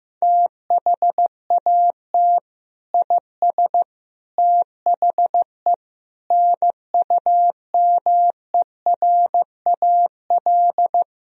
Morse code, heard as a keyed tone, about 15 wpm, 700 Hz. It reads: THAT IS THE NUMERAL
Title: Sentences from Top 1000 Words - Review Part 2 - 15wpm